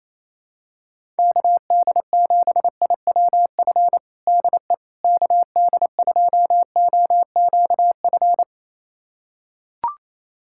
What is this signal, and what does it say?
Morse code, heard as a keyed tone, about 28 wpm, 700 Hz. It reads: KD7IWF DE KD2OQF
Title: USA Callsign DE Callsign - 6 Characters - 28wpm